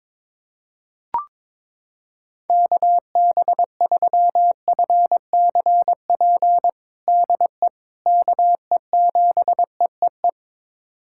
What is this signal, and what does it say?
Morse code, heard as a keyed tone, about 21 wpm, 700 Hz. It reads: KB3FCP DE KE7EEE
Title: USA Callsign DE Callsign 22wpm